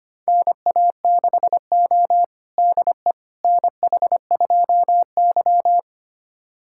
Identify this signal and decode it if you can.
Morse code, heard as a keyed tone, about 25 wpm, 700 Hz. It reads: NA6O DE NH2Y
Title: USA Callsign DE Callsign - 4 Characters - 25wpm